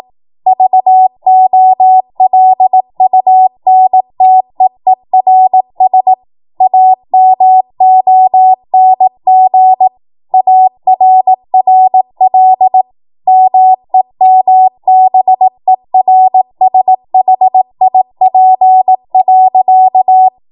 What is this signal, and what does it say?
Morse code, heard as a keyed tone, about 18 wpm, 750 Hz. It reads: VOLUNTEERS AMONG ARRL MEMBERSHIP.
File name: Morse / 18WPM